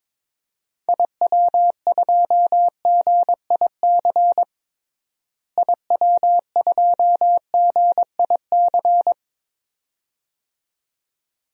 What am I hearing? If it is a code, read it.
Morse code, heard as a keyed tone, about 22 wpm, 700 Hz. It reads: IW2GIC IW2GIC
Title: DXCC-100 Call Signs - 6 Characters - Repeated 2x - 22wpm